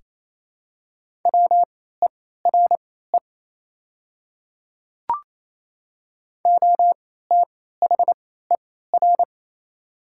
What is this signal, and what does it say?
Morse code, heard as a keyed tone, about 27 wpm, spaced as for 16 wpm, 690 Hz. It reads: WERE OTHER
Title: 3x Character Spacing - Top 100 Words - 28wpm